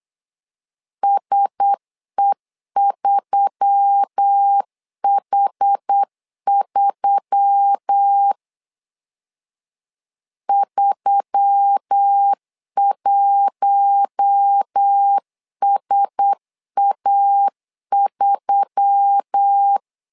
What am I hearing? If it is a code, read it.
Morse code, heard as a keyed tone, about 8 wpm, 800 Hz. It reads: SE3H3 31SA3
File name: LEZIONI / 4 Q1S3 8wpm 3037